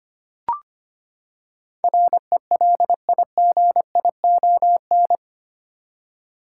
Morse code, heard as a keyed tone, about 25 wpm, 700 Hz. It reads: RELIGION
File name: Eight Letter Words - 25wpm